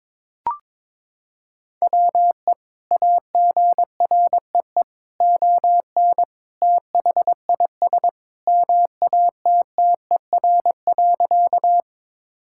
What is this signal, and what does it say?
Morse code, heard as a keyed tone, about 22 wpm, 700 Hz. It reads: WE AGREE ON THIS MATTER.
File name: practice / 22wpm / Sentences with 2-Word Phrases - 22wpm